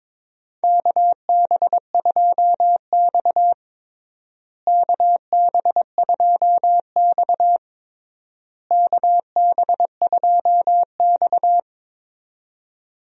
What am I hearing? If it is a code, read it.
Morse code, heard as a keyed tone, about 22 wpm, 700 Hz. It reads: KB2X KB2X KB2X
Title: USA Call Signs - 4 Characters - Repeated 3x - 22wpm